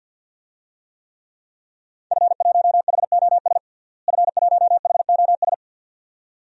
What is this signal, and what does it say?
Morse code, heard as a keyed tone, about 49 wpm, 700 Hz. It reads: F1HOS F1HOS
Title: DXCC-100 Call Signs - 5 Characters - Repeated 2x - 50wpm